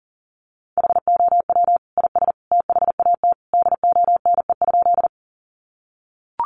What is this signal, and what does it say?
Morse code, heard as a keyed tone, about 40 wpm, 700 Hz. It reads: HOW IS THAT DONE?
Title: Sentences from Top 500 Words 40wpm